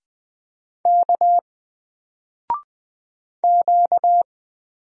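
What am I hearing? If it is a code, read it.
Morse code, heard as a keyed tone, about 20 wpm, 700 Hz. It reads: K Q
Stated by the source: Single letter and number Morse drill at 20 wpm, from N4LCD.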